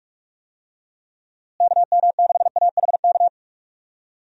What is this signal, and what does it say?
Morse code, heard as a keyed tone, about 47 wpm, 700 Hz. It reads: KM6AHK
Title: USA Call Signs - 6 Characters - Repeated 3x - 45wpm